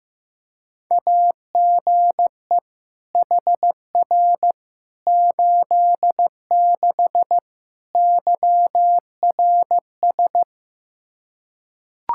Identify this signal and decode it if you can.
Morse code, heard as a keyed tone, about 15 wpm, 700 Hz. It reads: AGE HR 86 YRS
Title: Practice QSO - Exchange 3 - More Info - 15wpm